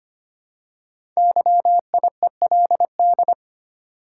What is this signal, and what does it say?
Morse code, heard as a keyed tone, about 25 wpm, 700 Hz. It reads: YIELD
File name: 5 Letter Words - First 3 Letters Given - 25wpm